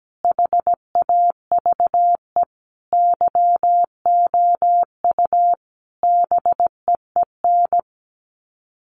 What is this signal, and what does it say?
Morse code, heard as a keyed tone, about 17 wpm, 700 Hz. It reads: HAVE YOU BEEN
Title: Sets of 5 Words Encore 17wpm